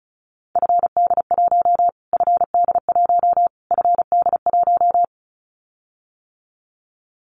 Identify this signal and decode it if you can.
Morse code, heard as a keyed tone, about 34 wpm, 700 Hz. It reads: FD1 FD1 FD1
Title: Callsign Prefixes for Top 25 DXCC - Repeated 3x - 35wpm